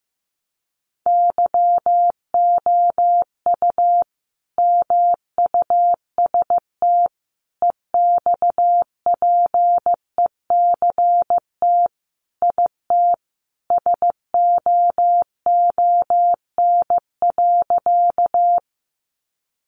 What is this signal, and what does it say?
Morse code, heard as a keyed tone, about 14 wpm, 700 Hz. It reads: YOU MUST EXPECT IT SOON.
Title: Sentences from Top 1100 Words 15wpm